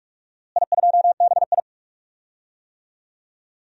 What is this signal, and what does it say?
Morse code, heard as a keyed tone, about 46 wpm, 700 Hz. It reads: I2BI
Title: DXCC-100 Call Signs - 4 Characters - Repeated 2x - 45wpm